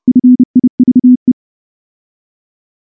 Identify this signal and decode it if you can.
Morse code, heard as a keyed tone, about 28 wpm, 250 Hz. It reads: FIVE
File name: morse / morse-training-30WPM